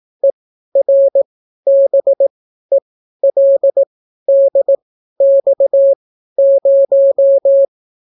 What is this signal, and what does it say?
Morse code, heard as a keyed tone, about 18 wpm, spaced as for 12 wpm, 550 Hz. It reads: ERBELDX0